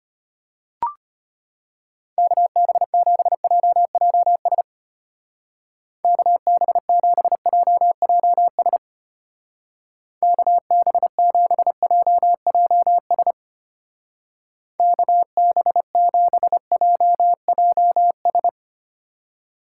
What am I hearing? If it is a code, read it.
Morse code, heard as a keyed tone, about 32 wpm, 700 Hz. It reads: KB7JJS KB7JJS KB7JJS KB7JJS